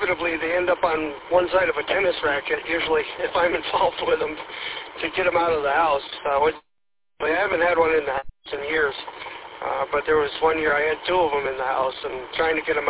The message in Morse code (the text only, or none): none